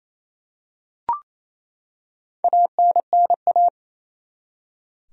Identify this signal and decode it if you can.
Morse code, heard as a keyed tone, about 28 wpm, 700 Hz. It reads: ANNA
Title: USA Common Names - 28wpm